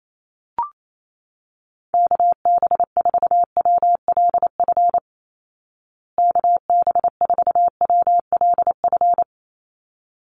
Morse code, heard as a keyed tone, about 28 wpm, 700 Hz. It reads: KB4WLF KB4WLF
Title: Call Signs Repeated 3x - 28wpm